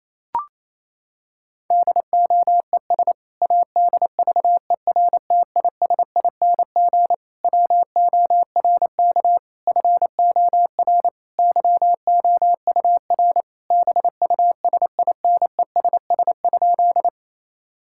Morse code, heard as a keyed tone, about 28 wpm, 700 Hz. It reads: DOES ADVERTISING WORK FOR YOUR BUSINESS?
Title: Sentences from Top 1000 Words - Review Part 6 - 28wpm